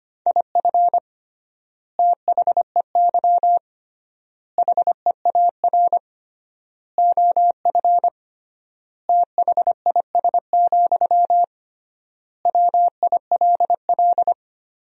Morse code, heard as a keyed tone, about 25 wpm, 700 Hz. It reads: IF THEY HEAR OF THIS, WILL
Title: 3x Word-Spacing - Sentences from Top 400 Words - 25wpm